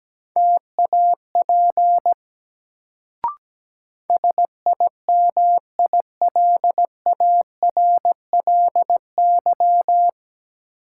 Morse code, heard as a keyed tone, about 17 wpm, 700 Hz. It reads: TAP SIMILARLY
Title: Sentences from Top 1400 Words - New Word Drill Alt - 17wpm